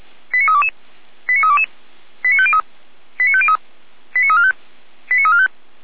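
Five-tone selective call (ZVEI), heard as quick runs of five tones, tones of about 70 ms. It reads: 89322 89322 89503 89503 89345 89345
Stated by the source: ZVEI five-tone selective call received with an SDR.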